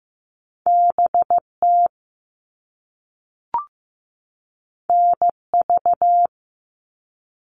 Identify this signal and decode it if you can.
Morse code, heard as a keyed tone, about 15 wpm, 700 Hz. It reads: BT NV
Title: All Cut Numbers - 2-Digit 15wpm